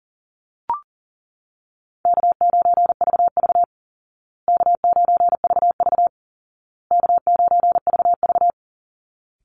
Morse code, heard as a keyed tone, about 40 wpm, 700 Hz. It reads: K9VV K9VV K9VV
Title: Easy Call Signs Repeated 3x - 40wpm